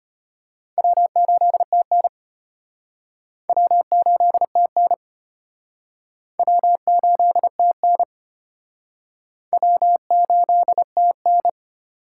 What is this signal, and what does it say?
Morse code, heard as a keyed tone, about 33 wpm, 700 Hz. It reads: W8TN W8TN W8TN W8TN